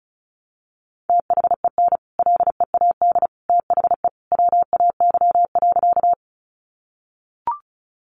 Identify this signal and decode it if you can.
Morse code, heard as a keyed tone, about 34 wpm, 700 Hz. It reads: THEN LEAD THE WAY.
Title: Sentences from Top 500 Words 35wpm